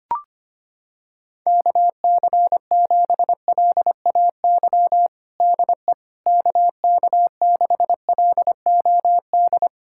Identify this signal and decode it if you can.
Morse code, heard as a keyed tone, about 25 wpm, 700 Hz. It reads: KC7LAY DE KK6LOD
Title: USA Callsign DE Callsign - 6 Characters - 25wpm